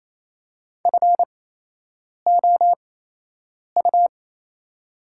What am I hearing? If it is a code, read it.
Morse code, heard as a keyed tone, about 28 wpm, spaced as for 8 wpm, 700 Hz. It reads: FOU
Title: 8x Character Spacing - Top 300 Words - 28wpm